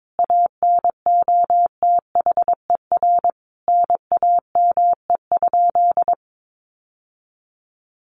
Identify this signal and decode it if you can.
Morse code, heard as a keyed tone, about 22 wpm, 700 Hz. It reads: ANOTHER NAME?
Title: Sentences from Top 600 Words 22wpm